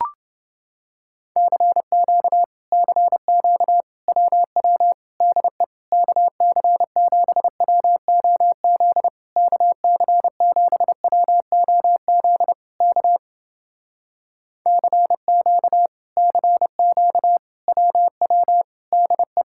Morse code, heard as a keyed tone, about 30 wpm, 700 Hz. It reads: CQ CQ WW DE KC7WOZ KC7WOZ K CQ CQ WW DE